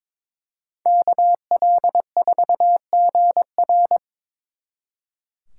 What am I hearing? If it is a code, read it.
Morse code, heard as a keyed tone, about 26 wpm, 700 Hz. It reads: KL4GR
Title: Speed Racing - USA 2x2 Call Signs 22wpm